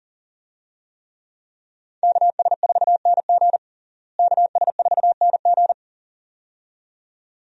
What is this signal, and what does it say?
Morse code, heard as a keyed tone, about 40 wpm, 700 Hz. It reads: KS4NG KS4NG
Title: USA Call Signs - 5 Characters - Repeated 2x - 40wpm